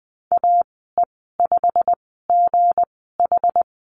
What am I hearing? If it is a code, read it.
Morse code, heard as a keyed tone, about 20 wpm, spaced as for 15 wpm, 700 Hz. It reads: AE5GH
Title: 245 Lesson 39g - Letter z - Call Signs 20wpm